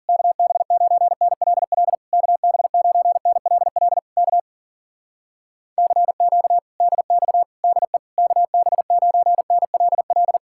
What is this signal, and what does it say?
Morse code, heard as a keyed tone, about 52 wpm, 700 Hz. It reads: KB9NLL KB9NLL K CQ DX DE KB9NLL